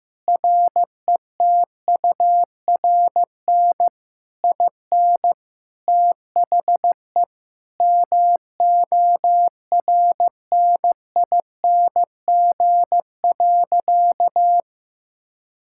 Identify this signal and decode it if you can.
Morse code, heard as a keyed tone, about 15 wpm, 700 Hz. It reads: RETURN IN THE MORNING.
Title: Sentences from Top 1000 Words - Review Part 6 - 15wpm